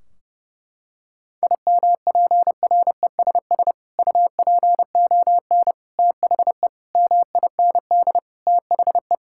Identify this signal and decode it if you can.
Morse code, heard as a keyed tone, about 30 wpm, 700 Hz. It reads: IMPRESS UPON THE MIND THE
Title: Sentences from Top 1600 Words 30wpm